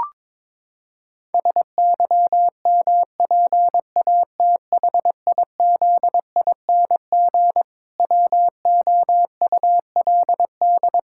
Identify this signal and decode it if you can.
Morse code, heard as a keyed tone, about 22 wpm, 700 Hz. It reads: SYMPATHIZING WOULD
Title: Pangram Challenge v2 - 22wpm